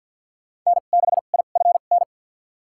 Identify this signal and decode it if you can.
Morse code, heard as a keyed tone, about 50 wpm, spaced as for 36 wpm, 700 Hz. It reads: N6IFN
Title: USA Call Signs - 2x Character Spacing - 50wpm